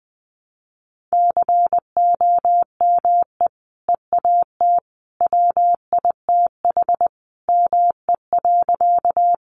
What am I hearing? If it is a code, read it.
Morse code, heard as a keyed tone, about 20 wpm, 700 Hz. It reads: COME EAT WITH ME.
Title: Sentences from Top 300 Words 20wpm